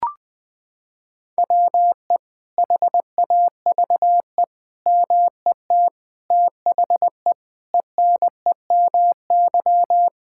Morse code, heard as a keyed tone, about 20 wpm, 700 Hz. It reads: WE HAVE MET THE ENEMY
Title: Sets of 9 Words - 20wpm